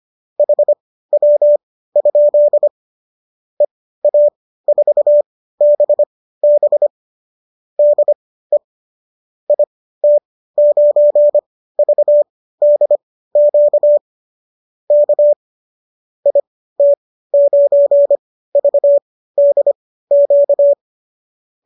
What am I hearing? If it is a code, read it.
Morse code, heard as a keyed tone, about 25 wpm, spaced as for 15 wpm, 590 Hz. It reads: HW? EA4BB DE IT9VDQ K IT9VDQ